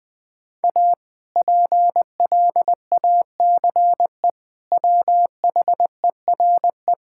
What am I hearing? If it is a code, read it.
Morse code, heard as a keyed tone, about 21 wpm, 700 Hz. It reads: A PLACE WHERE
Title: Sentences from Top 1000 Words - Review Part 5 - 20wpm